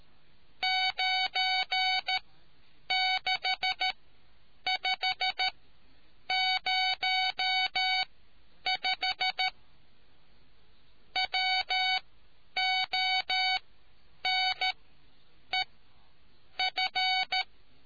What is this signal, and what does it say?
Morse code, heard as a keyed tone, about 13 wpm, spaced as for 8 wpm, 750 Hz. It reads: 96505 WONEF